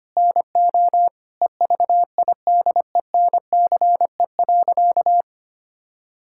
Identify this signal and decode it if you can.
Morse code, heard as a keyed tone, about 25 wpm, 700 Hz. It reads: NO EVIDENCE.
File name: practice / 25wpm / Sentences with 4-Word Phrases - 25wpm